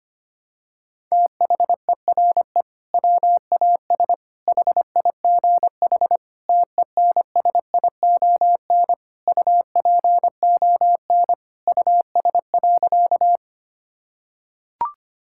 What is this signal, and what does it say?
Morse code, heard as a keyed tone, about 25 wpm, 700 Hz. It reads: THERE WAS HIGH TENSION UPON US.
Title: Sentences from Top 1600 Words 25wpm